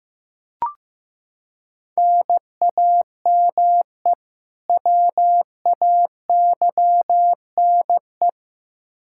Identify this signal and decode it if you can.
Morse code, heard as a keyed tone, about 15 wpm, 700 Hz. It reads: NAME WAYNE
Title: Practice QSO - Exchange 1 - Beginning - 15wpm